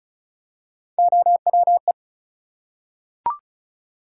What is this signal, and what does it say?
Morse code, heard as a keyed tone, about 36 wpm, 700 Hz. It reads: OWE
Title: Three Letter Words - 35wpm